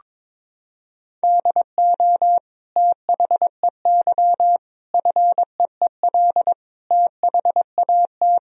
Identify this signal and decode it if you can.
Morse code, heard as a keyed tone, about 21 wpm, 700 Hz. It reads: DO THEY FEEL THAT
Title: Sentences from Top 1000 Words - Review Part 1 - 22wpm